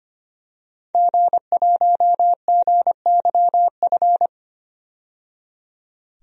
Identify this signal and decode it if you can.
Morse code, heard as a keyed tone, about 25 wpm, 700 Hz. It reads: G1GYF
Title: DXCC-100 Call Signs - 5 Characters - 25wpm